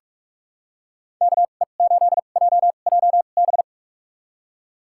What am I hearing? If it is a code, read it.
Morse code, heard as a keyed tone, about 44 wpm, spaced as for 32 wpm, 700 Hz. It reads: KE8JJB
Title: USA Call Signs - 6 Characters - 2x Character Spacing - 45wpm